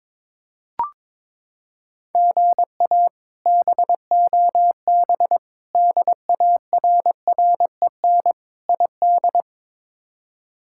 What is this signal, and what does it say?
Morse code, heard as a keyed tone, about 21 wpm, 700 Hz. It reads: GA BOB DARREN ID
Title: K1USN SST QSO Practice - 22wpm